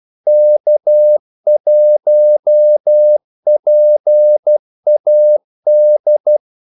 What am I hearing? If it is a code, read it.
Morse code, heard as a keyed tone, about 12 wpm, 600 Hz. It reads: K1PAD
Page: Kaldesignaler 12 wpm